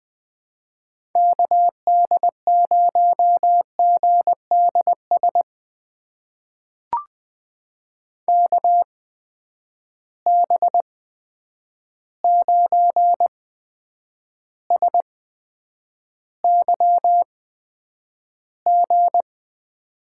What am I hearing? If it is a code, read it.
Morse code, heard as a keyed tone, about 20 wpm, spaced as for 9 wpm, 700 Hz. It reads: KD0GDS KB9SYG
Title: USA Call Signs - 6 Characters - 8x Character Spacing - 20wpm